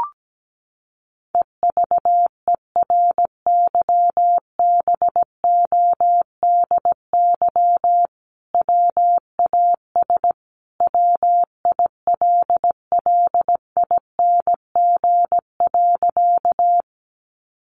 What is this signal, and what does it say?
Morse code, heard as a keyed tone, about 17 wpm, 700 Hz. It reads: EVERYBODY WAS WILLING.
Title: Sentences from Top 1400 Words 17wpm